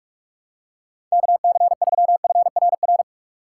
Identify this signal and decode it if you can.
Morse code, heard as a keyed tone, about 44 wpm, 700 Hz. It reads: KC3FRR
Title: USA Call Signs - 3x Character Spacing - 45wpm